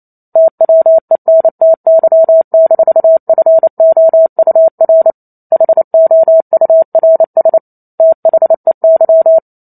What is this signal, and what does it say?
Morse code, heard as a keyed tone, about 28 wpm, 650 Hz. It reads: TWENTY-FOUR HOURS THEY